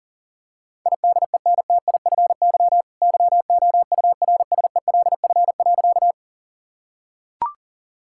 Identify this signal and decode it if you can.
Morse code, heard as a keyed tone, about 40 wpm, 700 Hz. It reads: IDENTIFY YOURSELF.
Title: Sentences from Top 1000 Words - Review Part 6 - 40wpm